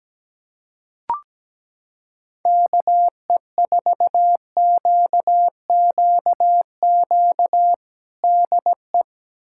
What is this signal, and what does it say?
Morse code, heard as a keyed tone, about 17 wpm, 700 Hz. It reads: KE4QQQ DE
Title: USA Callsign DE Callsign - 6 Characters - 17wpm